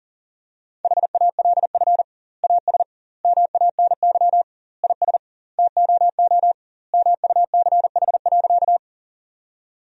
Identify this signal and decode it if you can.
Morse code, heard as a keyed tone, about 40 wpm, 700 Hz. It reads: HALF AS MANY IS TOO MUCH.
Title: 2x Word-Spacing - Sentences from Top 400 Words - 40wpm